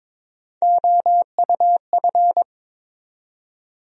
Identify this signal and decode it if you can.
Morse code, heard as a keyed tone, about 22 wpm, 700 Hz. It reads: OUF
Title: USA Call Sign Postfixes - 3 Letters 22wpm